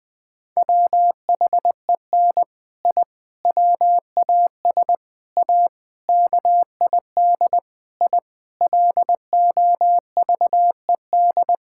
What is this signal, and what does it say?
Morse code, heard as a keyed tone, about 20 wpm, 700 Hz. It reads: WHEN I WAS A KID I LOVED
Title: Sentences with 5-Word Phrases - 20wpm